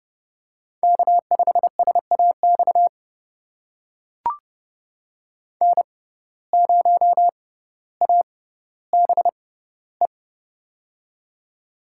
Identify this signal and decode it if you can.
Morse code, heard as a keyed tone, about 30 wpm, spaced as for 15 wpm, 700 Hz. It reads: K5SAX N0ABE
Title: USA Call Signs - 5 Characters - 6x Character Spacing - 30wpm